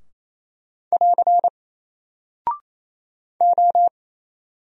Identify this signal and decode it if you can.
Morse code, heard as a keyed tone, about 27 wpm, 700 Hz. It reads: <AR> O
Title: Single Character (Letter:Number:Punctuation:Prosign) 28wpm